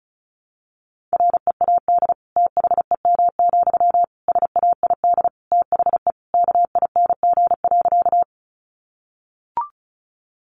Morse code, heard as a keyed tone, about 34 wpm, 700 Hz. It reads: READ THEM, SAID THE KING.
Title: Sentences from Top 400 Words 35wpm